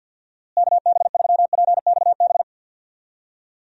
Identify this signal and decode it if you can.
Morse code, heard as a keyed tone, about 50 wpm, 700 Hz. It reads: KB3PXB
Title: USA Call Signs - 6 Characters - Repeated 2x - 50wpm